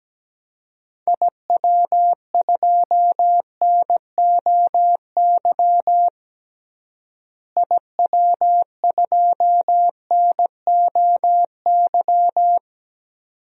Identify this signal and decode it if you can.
Morse code, heard as a keyed tone, about 17 wpm, 700 Hz. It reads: IW2NOY IW2NOY
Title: DXCC-100 Call Signs - 6 Characters - Repeated 2x - 17wpm